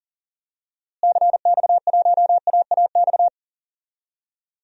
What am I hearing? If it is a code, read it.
Morse code, heard as a keyed tone, about 40 wpm, 700 Hz. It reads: CX1AAX
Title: DXCC-100 Call Signs - 6 Characters - Repeated 3x - 40wpm